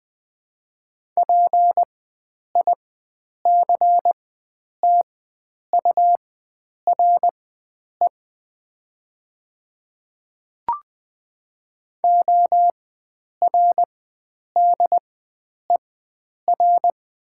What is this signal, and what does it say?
Morse code, heard as a keyed tone, about 20 wpm, spaced as for 9 wpm, 700 Hz. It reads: PICTURE ORDER
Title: 4x Character Spacing - Top 500 Words - 20wpm